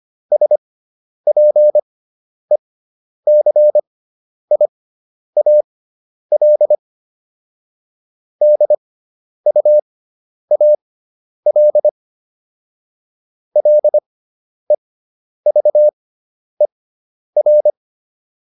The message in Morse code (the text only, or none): SPECIAL DUAL LEVER